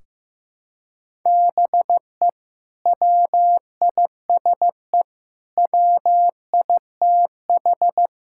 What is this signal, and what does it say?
Morse code, heard as a keyed tone, about 15 wpm, 700 Hz. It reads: BE WISE WITH
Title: Sentences from Top 1600 Words 15wpm